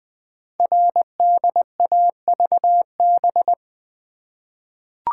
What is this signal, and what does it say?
Morse code, heard as a keyed tone, about 20 wpm, 700 Hz. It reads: RDAVB
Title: Code Groups - 5 Letters - 20wpm